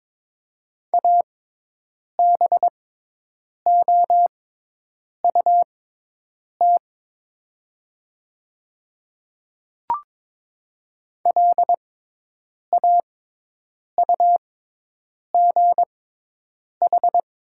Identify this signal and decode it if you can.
Morse code, heard as a keyed tone, about 22 wpm, spaced as for 8 wpm, 700 Hz. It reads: ABOUT LAUGH
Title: 6x Character Spacing - Top 500 Words - 22wpm